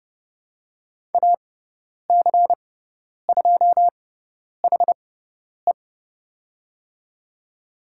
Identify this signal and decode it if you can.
Morse code, heard as a keyed tone, about 30 wpm, spaced as for 10 wpm, 700 Hz. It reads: AC2HE
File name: Lesson 16g - Letter c - Call Signs 30wpm